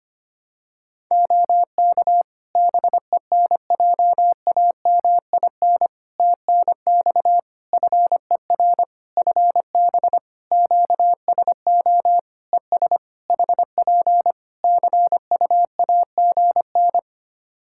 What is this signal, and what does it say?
Morse code, heard as a keyed tone, about 25 wpm, 700 Hz. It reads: OK BENJAMIN TNX FER FB QSO ES HP CUAGN
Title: Practice QSO - Exchange 4 - Ending - 25wpm